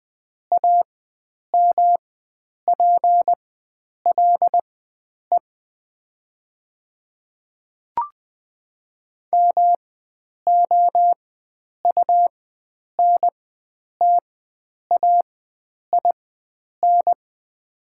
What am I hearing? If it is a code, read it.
Morse code, heard as a keyed tone, about 20 wpm, spaced as for 9 wpm, 700 Hz. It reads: AMPLE MOUNTAIN
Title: 4x Character Spacing - Top 300 Words - 20wpm